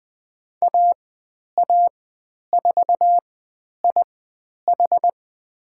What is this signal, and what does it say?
Morse code, heard as a keyed tone, about 20 wpm, spaced as for 10 wpm, 700 Hz. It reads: AA4IH
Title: 102 Lesson 17g - Letter u - Call Signs 20wpm